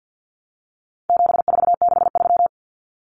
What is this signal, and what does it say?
Morse code, heard as a keyed tone, about 48 wpm, 700 Hz. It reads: 7463
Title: Common Cut Numbers - 4-Digit 50wpm